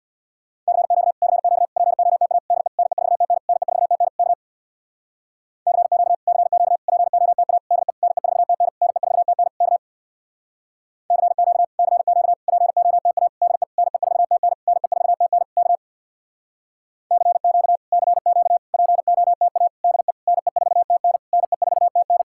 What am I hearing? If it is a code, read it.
Morse code, heard as a keyed tone, about 62 wpm, 700 Hz. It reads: CQ CQ POTA DE NE4TN NE4TN K CQ CQ POTA DE NE4TN NE4TN K CQ CQ POTA DE NE4TN NE4TN K CQ CQ POTA DE NE4TN NE4TN
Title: Speed-Racing Common QSO Phrases - 50wpm